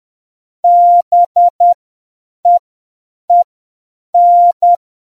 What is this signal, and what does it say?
Morse code, heard as a keyed tone, about 10 wpm, spaced as for 7 wpm, 700 Hz. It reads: BEEN